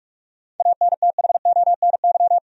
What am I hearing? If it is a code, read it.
Morse code, heard as a keyed tone, about 46 wpm, 700 Hz. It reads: ANTHONY
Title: USA Common Names - 45wpm